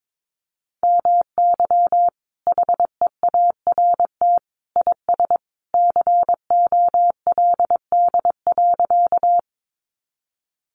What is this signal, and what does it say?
Morse code, heard as a keyed tone, about 22 wpm, 700 Hz. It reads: MY HEART IS COLD.
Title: Sentences from Top 1200 Words 22wpm